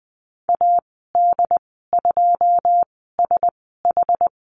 Morse code, heard as a keyed tone, about 20 wpm, spaced as for 15 wpm, 700 Hz. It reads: AD2SH
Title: 082 Lesson 14g - Number 2 - Call Signs 20wpm